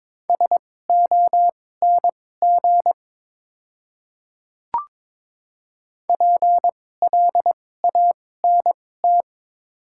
Morse code, heard as a keyed tone, about 22 wpm, spaced as for 16 wpm, 690 Hz. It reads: SONG PLANT
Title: 2x Character Spacing - Top 500 Words - 22wpm